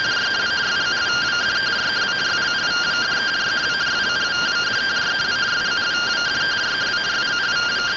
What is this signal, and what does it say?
Two-tone teleprinter signal, about 170 Hz shift.